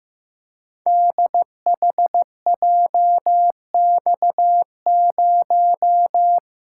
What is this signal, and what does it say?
Morse code, heard as a keyed tone, about 15 wpm, 700 Hz. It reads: DHJX0
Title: Code Groups - 5 Letter:Numbers - 15wpm